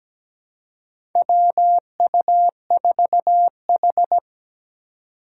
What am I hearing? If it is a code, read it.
Morse code, heard as a keyed tone, about 17 wpm, 700 Hz. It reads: WU4H
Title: USA Call Signs - 2x Character Spacing - 17wpm